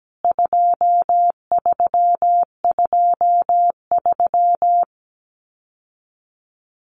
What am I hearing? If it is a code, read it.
Morse code, heard as a keyed tone, about 17 wpm, 700 Hz. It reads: 2323
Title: CWT Exchange 17wpm